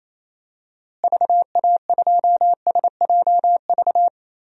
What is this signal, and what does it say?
Morse code, heard as a keyed tone, about 28 wpm, 700 Hz. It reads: VA2SJV
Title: DXCC-100 Call Signs - 6 Characters - Repeated 2x - 28wpm